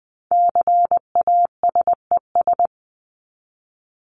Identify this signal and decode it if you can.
Morse code, heard as a keyed tone, about 20 wpm, 700 Hz. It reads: CASES
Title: ES:S Suffix - Top 100 Words with Base Words 20wpm